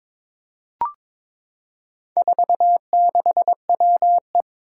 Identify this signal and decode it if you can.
Morse code, heard as a keyed tone, about 22 wpm, 700 Hz. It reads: 46WE
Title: Four Challenging Letter-Number Combinations 22wpm